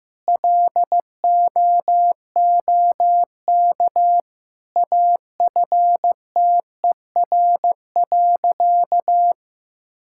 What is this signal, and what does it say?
Morse code, heard as a keyed tone, about 15 wpm, 700 Hz. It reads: LOOK AFTER.
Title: Sentences from Top 1600 Words 15wpm